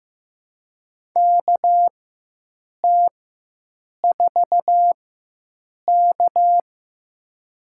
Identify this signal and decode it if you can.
Morse code, heard as a keyed tone, about 15 wpm, spaced as for 7 wpm, 700 Hz. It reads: KT4K
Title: USA Call Signs - 4 Characters - 4x Character Spacing - 15wpm